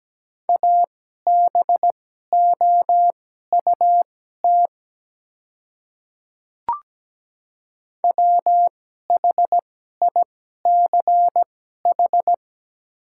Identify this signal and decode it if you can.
Morse code, heard as a keyed tone, about 17 wpm, spaced as for 12 wpm, 700 Hz. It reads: ABOUT WHICH
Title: 2x Character Spacing - Top 200 Words - 17wpm